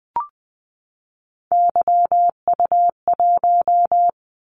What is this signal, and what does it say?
Morse code, heard as a keyed tone, about 20 wpm, 700 Hz. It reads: YU1
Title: Callsign Prefixes for Top 25 DXCC - 20wpm